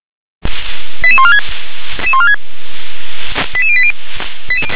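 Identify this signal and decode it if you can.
Five-tone selective call (ZVEI), heard as quick runs of five tones, tones of about 80 ms. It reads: 88146 88080
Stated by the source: Five-tone ZVEI selective call received on an SDR.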